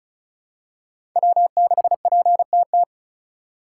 Morse code, heard as a keyed tone, about 36 wpm, 700 Hz. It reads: W6PTT
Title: USA Call Signs - 5 Characters - 2x Character Spacing - 35wpm